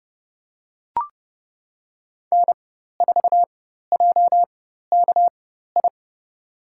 Morse code, heard as a keyed tone, about 32 wpm, spaced as for 14 wpm, 700 Hz. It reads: N4JKI
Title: USA Call Signs - 5 Characters - 4x Character Spacing - 30wpm